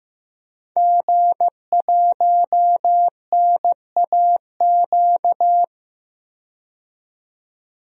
Morse code, heard as a keyed tone, about 15 wpm, 700 Hz. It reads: G1NAQ
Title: DXCC-100 Call Signs - 5 Characters - 15wpm